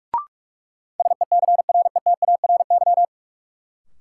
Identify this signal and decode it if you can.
Morse code, heard as a keyed tone, about 46 wpm, 700 Hz. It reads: SECRETARY
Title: Sentences from Top 1300 Words - New Word Drill Alt - 45wpm